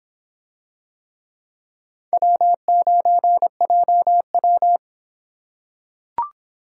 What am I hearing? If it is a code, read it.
Morse code, heard as a keyed tone, about 24 wpm, 700 Hz. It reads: W9JW